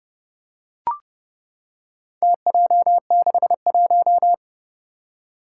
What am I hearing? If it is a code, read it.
Morse code, heard as a keyed tone, about 30 wpm, 700 Hz. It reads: TJ61
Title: Four Challenging Letter-Number Combinations 30wpm